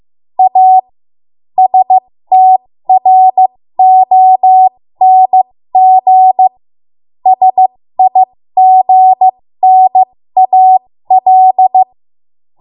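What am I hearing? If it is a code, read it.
Morse code, heard as a keyed tone, about 15 wpm, spaced as for 13 wpm, 750 Hz. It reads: A STRONG SIGNAL
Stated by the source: 13 wpm Morse code practice archive transmission.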